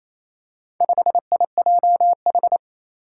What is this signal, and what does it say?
Morse code, heard as a keyed tone, about 27 wpm, 700 Hz. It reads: 5IJH
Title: Four Challenging Letter-Number Combinations 28wpm